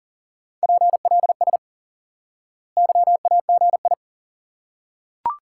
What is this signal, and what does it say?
Morse code, heard as a keyed tone, about 40 wpm, 700 Hz. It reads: PLS YAGI
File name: IQE Course 3 - Lesson 89 - YAGI - Two Elements - 40wpm